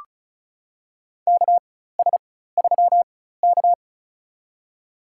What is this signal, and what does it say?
Morse code, heard as a keyed tone, about 36 wpm, spaced as for 16 wpm, 700 Hz. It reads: KS3K
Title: USA Call Signs - 4 Characters - 4x Character Spacing - 35wpm